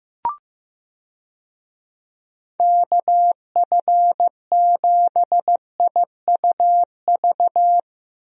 Morse code, heard as a keyed tone, about 15 wpm, 700 Hz. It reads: KF7IUV
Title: USA Call Signs - 6 Characters - Repeated 2x - 15wpm